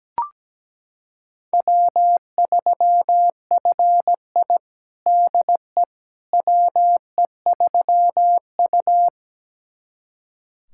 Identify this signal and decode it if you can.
Morse code, heard as a keyed tone, about 17 wpm, 700 Hz. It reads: W3FI DE WE3U
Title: USA Callsign DE Callsign - 4 Characters - 17wpm